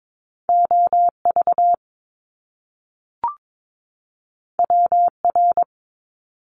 Morse code, heard as a keyed tone, about 22 wpm, 700 Hz. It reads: OV WR
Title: Top Two Letter Combinations in Words 22wpm